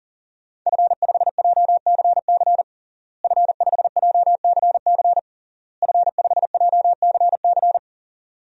Morse code, heard as a keyed tone, about 37 wpm, 700 Hz. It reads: F5JCC F5JCC F5JCC
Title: DXCC-100 Call Signs - 5 Characters - Repeated 3x - 40wpm